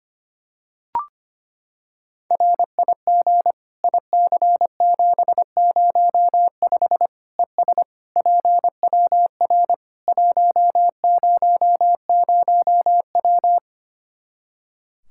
Morse code, heard as a keyed tone, about 25 wpm, 700 Hz. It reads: RIG IC705 ES PWR 100W
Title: Practice QSO - Exchange 2 - Info - 25wpm